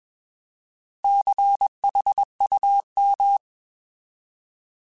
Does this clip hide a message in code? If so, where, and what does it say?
Yes: Morse code from 1.0 to 3.4 s, heard as a keyed tone, about 21 wpm, 800 Hz. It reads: CHUM